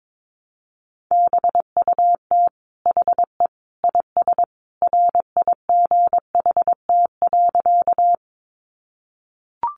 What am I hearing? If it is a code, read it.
Morse code, heard as a keyed tone, about 22 wpm, 700 Hz. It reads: BUT HE IS RIGHT.
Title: Sentences from Top 200 Words 22wpm